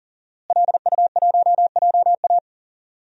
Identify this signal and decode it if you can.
Morse code, heard as a keyed tone, about 40 wpm, 700 Hz. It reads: LU1JA
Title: DXCC-100 Call Signs - 5 Characters - Repeated 3x - 40wpm